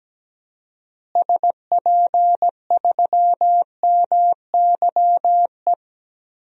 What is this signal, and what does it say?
Morse code, heard as a keyed tone, about 17 wpm, 700 Hz. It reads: SP3MYE
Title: DXCC-100 Call Signs - 6 Characters - Repeated 3x - 17wpm